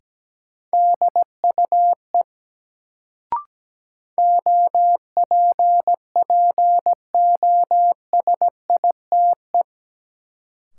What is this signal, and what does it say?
Morse code, heard as a keyed tone, about 17 wpm, 700 Hz. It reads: DUE OPPOSITE
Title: Sets of 5 Words Encore - New Word Drill - 17wpm